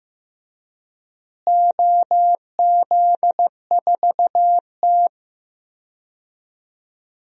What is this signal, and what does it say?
Morse code, heard as a keyed tone, about 15 wpm, 700 Hz. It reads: OZ4T
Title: DXCC-100 Call Signs - 4 Characters - Repeated 2x - 15wpm